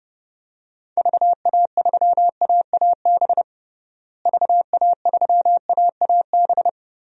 Morse code, heard as a keyed tone, about 30 wpm, 700 Hz. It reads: VA3AAB VA3AAB
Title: DXCC-100 Call Signs - 6 Characters - Repeated 3x - 30wpm